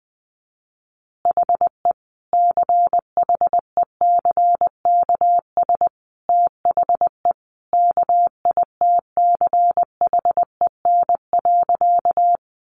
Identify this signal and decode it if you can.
Morse code, heard as a keyed tone, about 20 wpm, 700 Hz. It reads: HE CHECKS THE KITCHEN.
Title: Easy Phrases 20wpm